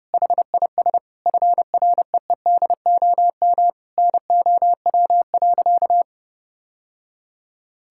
Morse code, heard as a keyed tone, about 31 wpm, 700 Hz. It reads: HIS FREEDOM NOW.
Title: Sentences from Top 1000 Words - Review Part 5 - 30wpm